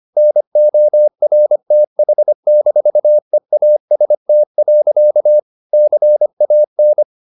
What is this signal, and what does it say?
Morse code, heard as a keyed tone, about 25 wpm, 600 Hz. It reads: NORTH-EAST. CAN
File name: the old man and the sea part 1 25wpm